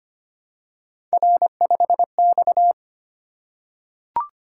Morse code, heard as a keyed tone, about 25 wpm, 700 Hz. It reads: R5X
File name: Three Letter-Number Combinations - 25wpm